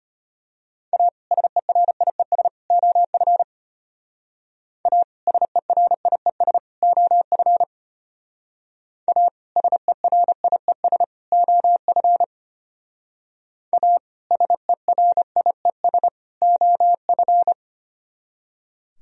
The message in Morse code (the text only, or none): A SERIES OF A SERIES OF A SERIES OF A SERIES OF